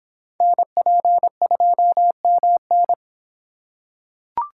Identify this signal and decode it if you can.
Morse code, heard as a keyed tone, about 22 wpm, 700 Hz. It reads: NP2MN